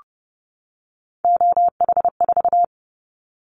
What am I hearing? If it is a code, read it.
Morse code, heard as a keyed tone, about 30 wpm, 700 Hz. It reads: OH4
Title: Callsign Prefixes for Top 25 DXCC - 30wpm